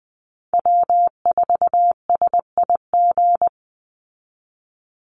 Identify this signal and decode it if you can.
Morse code, heard as a keyed tone, about 20 wpm, 700 Hz. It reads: W4SIG
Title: Call Signs Repeated 3x - 20wpm